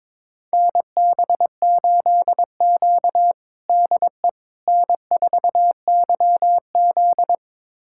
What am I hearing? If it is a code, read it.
Morse code, heard as a keyed tone, about 22 wpm, 700 Hz. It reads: NB8Q DE N4YZ
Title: USA Callsign DE Callsign - 4 Characters - 22wpm